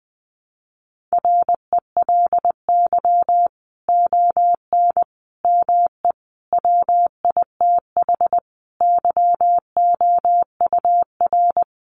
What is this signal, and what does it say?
Morse code, heard as a keyed tone, about 20 wpm, 700 Hz. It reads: RELY ON ME WITH YOUR
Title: Sentences from Top 1200 Words 20wpm